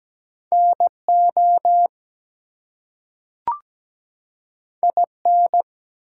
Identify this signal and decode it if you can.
Morse code, heard as a keyed tone, about 17 wpm, 700 Hz. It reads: NO IN
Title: Two Letter Words - 17wpm